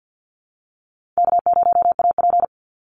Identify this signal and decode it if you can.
Morse code, heard as a keyed tone, about 50 wpm, 700 Hz. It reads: K0AP
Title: Easy Call Signs Repeated 3x - 50wpm